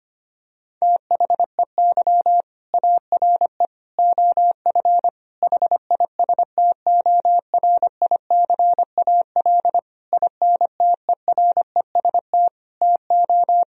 Morse code, heard as a keyed tone, about 25 wpm, 700 Hz. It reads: THEY ARE OF HISTORICAL INTEREST TO
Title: Sentences from Top 1000 Words - Review Part 4 - 25wpm